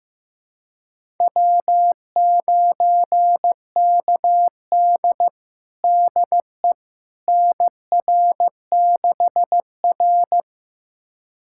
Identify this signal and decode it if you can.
Morse code, heard as a keyed tone, about 16 wpm, 700 Hz. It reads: W9KD DE NR6R
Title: USA Callsign DE Callsign - 4 Characters - 15wpm